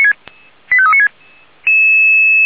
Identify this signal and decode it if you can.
Five-tone selective call (ZVEI), heard as quick runs of five tones, tones of about 70 ms. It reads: 85396
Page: SDR/ZVEI